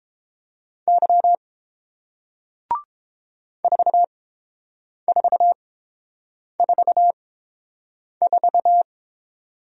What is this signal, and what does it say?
Morse code, heard as a keyed tone, about 30 wpm, spaced as for 7 wpm, 700 Hz. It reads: Y 4444